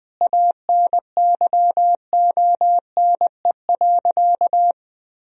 Morse code, heard as a keyed tone, about 20 wpm, 700 Hz. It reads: ANYONE.
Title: Sentences from Top 1000 Words - Review Part 2 - 20wpm